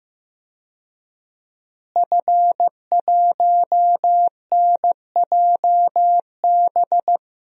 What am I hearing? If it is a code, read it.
Morse code, heard as a keyed tone, about 15 wpm, 700 Hz. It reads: F1NJB
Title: DXCC-100 Call Signs - 5 Characters - Repeated 2x - 15wpm